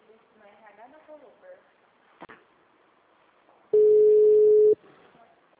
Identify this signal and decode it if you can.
Ringback tone (Europe, 425 Hz), with speech.
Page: Dracena 2 - SP